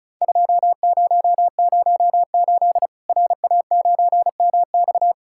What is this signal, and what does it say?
Morse code, heard as a keyed tone, about 36 wpm, 700 Hz. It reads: 2008 RA9MX
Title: Worked All Europe DX Contest - QTC Exchange - 35wpm